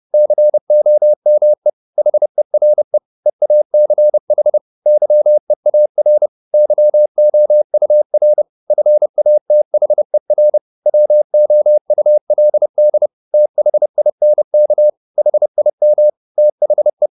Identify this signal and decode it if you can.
Morse code, heard as a keyed tone, about 30 wpm, 600 Hz. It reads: COME HERE EACH YEAR YOUR FATHER WOULD THINK HIM THE
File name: the old man and the sea / the old man and the sea 1 30wpm